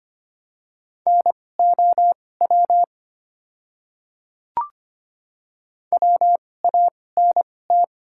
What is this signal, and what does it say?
Morse code, heard as a keyed tone, about 25 wpm, spaced as for 18 wpm, 700 Hz. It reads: NOW WANT
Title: 2x Character Spacing - Top 300 Words - 25wpm